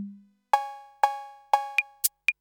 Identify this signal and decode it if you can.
There is no code.